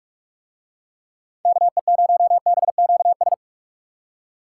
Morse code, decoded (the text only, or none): KE0BQS